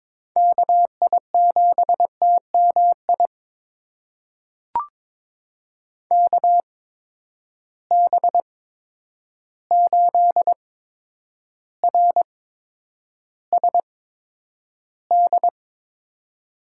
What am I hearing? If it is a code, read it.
Morse code, heard as a keyed tone, about 22 wpm, spaced as for 9 wpm, 700 Hz. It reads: KI7TMI KB8RSD
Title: USA Call Signs - 6 Characters - 8x Character Spacing - 22wpm